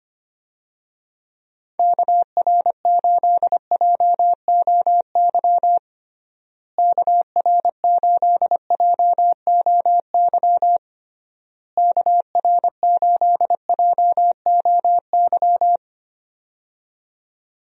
Morse code, heard as a keyed tone, about 25 wpm, 700 Hz. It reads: KR8JOY KR8JOY KR8JOY
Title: USA Call Signs - 6 Characters - Repeated 3x - 25wpm